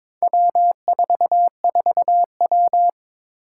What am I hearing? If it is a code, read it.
Morse code, heard as a keyed tone, about 22 wpm, 700 Hz. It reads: W44W
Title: Four Challenging Letter-Number Combinations 22wpm